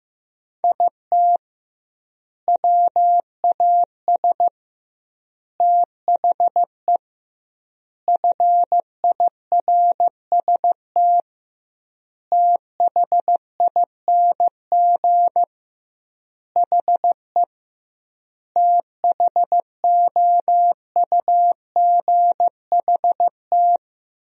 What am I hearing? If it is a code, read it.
Morse code, heard as a keyed tone, about 16 wpm, 700 Hz. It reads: IT WAS THE FIRST THING HE THOUGHT
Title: 2x Word-Spacing - Sentences from Top 300 Words - 15wpm